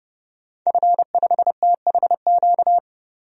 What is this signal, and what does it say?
Morse code, heard as a keyed tone, about 30 wpm, 700 Hz. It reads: F5THQ
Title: DXCC-100 Call Signs - 5 Characters - 30wpm